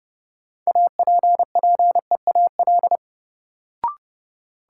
Morse code, heard as a keyed tone, about 31 wpm, 700 Hz. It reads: APPEAL
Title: Sentences from Top 1300 Words - New Word Drill - 30wpm